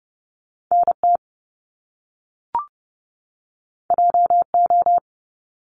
Morse code, heard as a keyed tone, about 30 wpm, 700 Hz. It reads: NT JO
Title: Top Two Letter Combinations in Words 30wpm